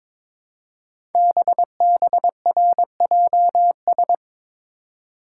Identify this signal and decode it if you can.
Morse code, heard as a keyed tone, about 22 wpm, 700 Hz. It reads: BBRJS
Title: Code Groups - 5 Letters - 22wpm